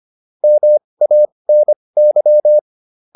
Morse code, heard as a keyed tone, about 25 wpm, spaced as for 20 wpm, 600 Hz. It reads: MANY